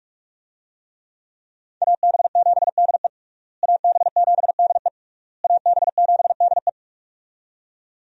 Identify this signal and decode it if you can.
Morse code, heard as a keyed tone, about 41 wpm, 700 Hz. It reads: AB7DE AB7DE AB7DE
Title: USA Call Signs - 5 Characters - Repeated 3x - 45wpm